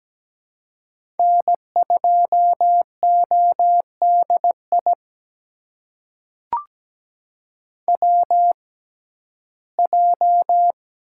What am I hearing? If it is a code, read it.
Morse code, heard as a keyed tone, about 17 wpm, spaced as for 12 wpm, 700 Hz. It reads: N2ODI WJ
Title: USA Call Signs - 6x Character Spacing - 17wpm